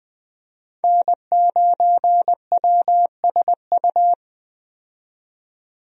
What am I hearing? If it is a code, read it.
Morse code, heard as a keyed tone, about 20 wpm, 700 Hz. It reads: N9WSU
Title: USA Call Signs - 5 Characters - 20wpm